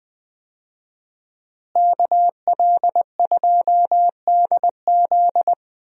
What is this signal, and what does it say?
Morse code, heard as a keyed tone, about 20 wpm, 700 Hz. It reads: KL2DZ
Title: USA Call Signs - 5 Characters - Repeated 2x - 20wpm